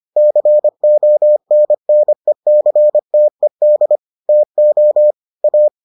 Morse code, heard as a keyed tone, about 25 wpm, 600 Hz. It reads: CONNECTED TO A